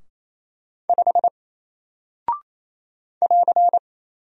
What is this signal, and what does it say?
Morse code, heard as a keyed tone, about 27 wpm, 700 Hz. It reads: 5 <AR>
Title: Single Character (Letter:Number:Punctuation:Prosign) 28wpm